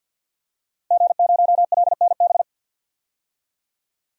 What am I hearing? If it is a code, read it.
Morse code, heard as a keyed tone, about 46 wpm, 700 Hz. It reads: G0LNB
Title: DXCC-100 Call Signs - 5 Characters - Repeated 3x - 50wpm